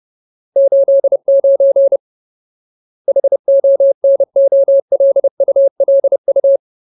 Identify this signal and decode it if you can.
Morse code, heard as a keyed tone, about 30 wpm, 550 Hz. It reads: 89 HONOLULU
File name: Ultimate Word List / 30WPM Ultimate